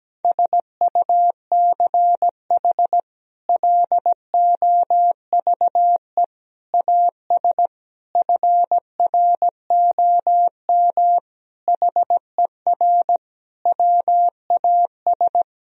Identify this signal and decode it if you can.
Morse code, heard as a keyed tone, about 17 wpm, 700 Hz. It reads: SUCH LOVE AS FROM HER WAS
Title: Sentences from Top 1000 Words - Review Part 1 - 17wpm